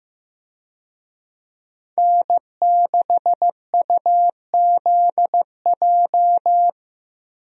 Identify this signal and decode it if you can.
Morse code, heard as a keyed tone, about 15 wpm, 700 Hz. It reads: N6UZJ
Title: USA Call Signs - 5 Characters - Repeated 3x - 15wpm